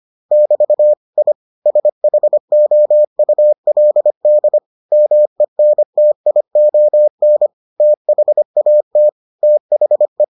Morse code, heard as a keyed tone, about 25 wpm, 600 Hz. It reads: <BT> I SHOULD MENTION THAT THE